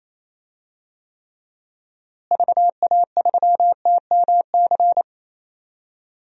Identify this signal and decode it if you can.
Morse code, heard as a keyed tone, about 28 wpm, 700 Hz. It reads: VA3TMC
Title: DXCC-100 Call Signs - 6 Characters - Repeated 2x - 28wpm